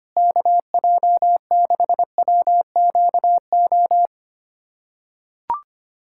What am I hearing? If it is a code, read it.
Morse code, heard as a keyed tone, about 25 wpm, 700 Hz. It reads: KJ6WQO
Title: USA Callsign DE Callsign 25wpm